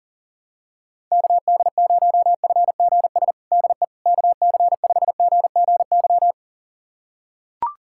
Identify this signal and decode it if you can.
Morse code, heard as a keyed tone, about 37 wpm, 700 Hz. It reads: KD0FGS DE KC5GGY
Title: USA Callsign DE Callsign - 6 Characters - 40wpm